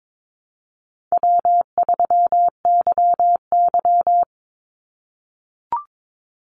Morse code, heard as a keyed tone, about 22 wpm, 700 Hz. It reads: W3YY
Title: Easy Call Signs - 22wpm